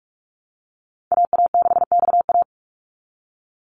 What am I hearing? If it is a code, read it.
Morse code, heard as a keyed tone, about 44 wpm, 700 Hz. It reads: AA6XA
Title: Call Signs Repeated 2x - 45wpm